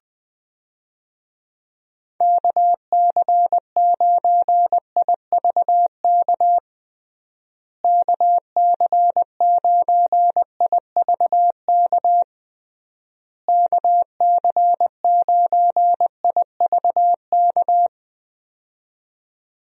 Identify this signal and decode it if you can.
Morse code, heard as a keyed tone, about 20 wpm, 700 Hz. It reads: KC9IVK KC9IVK KC9IVK
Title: USA Call Signs - 6 Characters - Repeated 3x - 20wpm